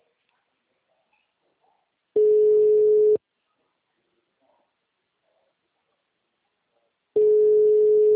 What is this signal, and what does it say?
Ringback tone (Europe, 425 Hz).
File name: out- -2014- - - .45769